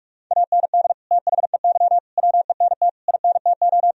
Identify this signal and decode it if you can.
Morse code, heard as a keyed tone, about 41 wpm, 700 Hz. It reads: AND THEY WENT INTO
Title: Sentences from Top 1000 Words - Review Part 2 - 45wpm